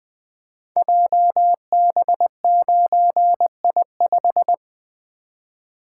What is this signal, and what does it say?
Morse code, heard as a keyed tone, about 20 wpm, 700 Hz. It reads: JB9I5
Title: Code Groups - 5 Letter:Numbers - 20wpm